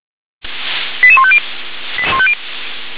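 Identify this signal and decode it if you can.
Five-tone selective call (ZVEI), heard as quick runs of five tones, tones of about 70 ms. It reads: 88150
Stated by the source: ZVEI five-tone call 88640 received by SDR.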